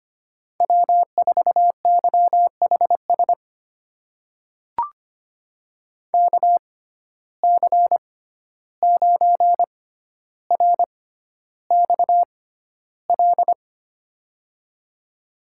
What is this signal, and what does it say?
Morse code, heard as a keyed tone, about 25 wpm, spaced as for 12 wpm, 700 Hz. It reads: W4YHS KC9RXL
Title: USA Call Signs - 6x Character Spacing - 25wpm